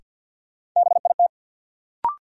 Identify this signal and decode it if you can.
Morse code, heard as a keyed tone, about 52 wpm, 700 Hz. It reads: BIT
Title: Sentences from Top 900 Words - New Word Drill - 50wpm